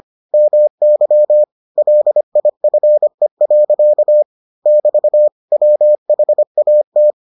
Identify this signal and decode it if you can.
Morse code, heard as a keyed tone, about 25 wpm, 600 Hz. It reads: MY LIFE. <BT> WHAT